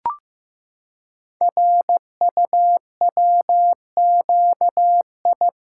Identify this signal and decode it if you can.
Morse code, heard as a keyed tone, about 15 wpm, 700 Hz. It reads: RUWQI
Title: Code Groups - 5 Letters - 15wpm